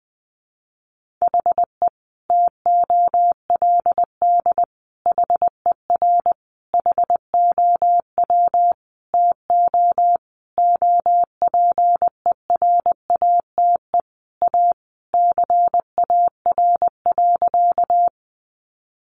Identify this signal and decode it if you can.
Morse code, heard as a keyed tone, about 20 wpm, 700 Hz. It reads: HE TOLD HER HOW TO OPERATE A CAR.
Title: Sentences from Top 1500 Words 20wpm